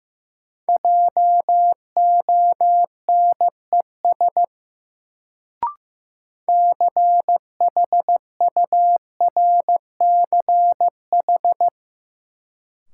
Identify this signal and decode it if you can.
Morse code, heard as a keyed tone, about 15 wpm, 700 Hz. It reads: JONES CHURCH
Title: Sets of 5 Words - New Word Drill - 15wpm